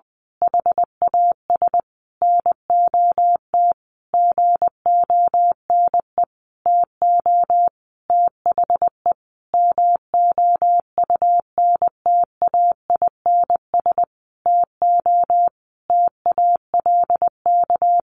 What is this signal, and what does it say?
Morse code, heard as a keyed tone, about 20 wpm, 700 Hz. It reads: HAS NOT GONE TO THE MOUNTAINS TO TALK
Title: A Sound Of Thunder 20wpm